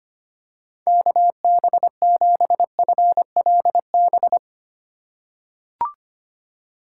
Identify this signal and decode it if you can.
Morse code, heard as a keyed tone, about 25 wpm, 700 Hz. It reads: KB7FLB
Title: USA Call Signs - 6 Characters - 6x Character Spacing - 25wpm